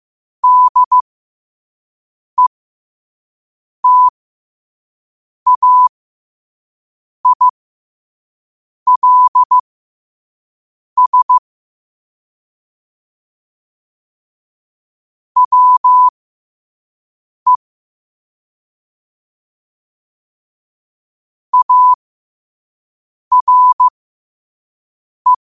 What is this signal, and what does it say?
Morse code, heard as a keyed tone, about 15 wpm, spaced as for 5 wpm, 1000 Hz. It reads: DETAILS WE ARE
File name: qotdfiles / 5wpm